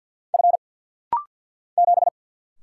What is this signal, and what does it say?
Morse code, heard as a keyed tone, about 49 wpm, 700 Hz. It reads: F 7
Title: Single Letter-Number Alt - Rapid-Fire 50wpm